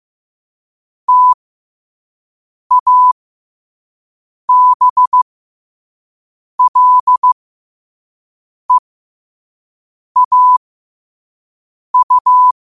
Morse code, heard as a keyed tone, about 15 wpm, spaced as for 5 wpm, 1000 Hz. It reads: TABLEAU
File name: qotdfiles / 5wpm